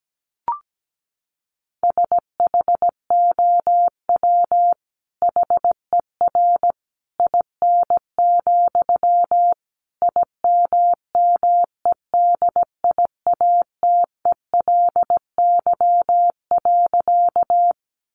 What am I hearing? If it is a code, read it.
Morse code, heard as a keyed tone, about 17 wpm, 700 Hz. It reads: SHOW HER IN, IMMEDIATELY.
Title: Sentences from Top 1100 Words 17wpm